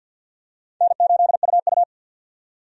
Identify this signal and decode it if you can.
Morse code, heard as a keyed tone, about 52 wpm, 700 Hz. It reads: N8UU
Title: USA Call Signs - 4 Characters - 50wpm